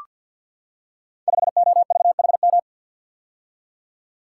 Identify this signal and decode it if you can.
Morse code, heard as a keyed tone, about 47 wpm, 700 Hz. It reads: 5OVHM
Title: Code Groups - 5 Letter:Numbers - 50wpm